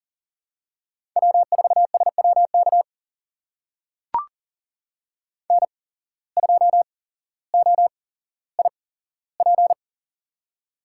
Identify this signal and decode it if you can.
Morse code, heard as a keyed tone, about 42 wpm, spaced as for 17 wpm, 700 Hz. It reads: W4SWK N2OIP
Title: USA Call Signs - 5 Characters - 8x Character Spacing - 40wpm